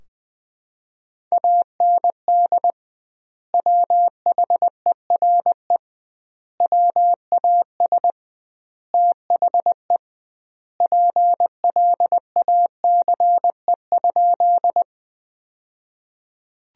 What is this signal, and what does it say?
Morse code, heard as a keyed tone, about 20 wpm, 700 Hz. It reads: AND WHERE WAS THE PLACE?
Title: 2x Word-Spacing - Sentences from Top 200 Words - 20wpm